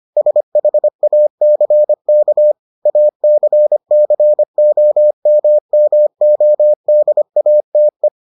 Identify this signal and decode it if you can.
Morse code, heard as a keyed tone, about 25 wpm, 600 Hz. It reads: SHACK ACCOMMODATE